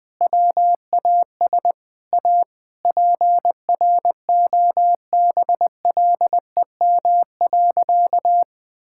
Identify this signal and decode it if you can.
Morse code, heard as a keyed tone, about 21 wpm, 700 Hz. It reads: WAS A PROBLEM.
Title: Sentences with 3-Word Phrases - 20wpm